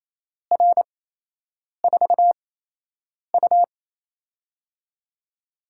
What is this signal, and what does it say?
Morse code, heard as a keyed tone, about 28 wpm, spaced as for 8 wpm, 700 Hz. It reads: R4U
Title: USA Call Signs - 8x Character Spacing - 28wpm